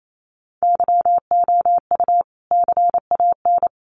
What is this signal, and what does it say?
Morse code, heard as a keyed tone, about 28 wpm, 700 Hz. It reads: YOU CAN
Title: Sentences from Top 1100 Words 28wpm